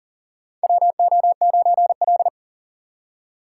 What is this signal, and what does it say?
Morse code, heard as a keyed tone, about 38 wpm, 700 Hz. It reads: WO9L
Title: USA Call Signs - 4 Characters - 3x Character Spacing - 40wpm